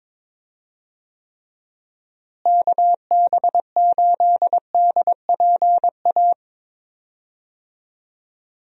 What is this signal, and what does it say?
Morse code, heard as a keyed tone, about 22 wpm, 700 Hz. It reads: KB8DPA
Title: USA Call Signs - 6 Characters - Repeated 2x - 22wpm